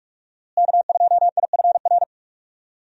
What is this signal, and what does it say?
Morse code, heard as a keyed tone, about 46 wpm, 700 Hz. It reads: K2IFR
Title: USA Call Signs - 5 Characters - Repeated 3x - 45wpm